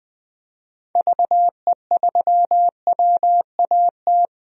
Morse code, heard as a keyed tone, about 20 wpm, 700 Hz. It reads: VE3WAT
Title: DXCC-100 Call Signs - 6 Characters - 20wpm